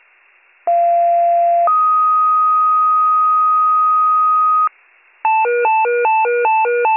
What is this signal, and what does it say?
Two-tone page, heard as a one-second tone, then a three-second tone, about 690 then 1250 Hz, with noise. It is heard from 0.7 to 4.7 s.